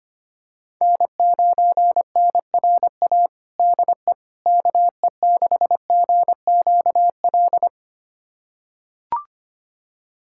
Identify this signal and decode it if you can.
Morse code, heard as a keyed tone, about 25 wpm, 700 Hz. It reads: N9NRA DE KE6GQL
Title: USA Callsign DE Callsign 25wpm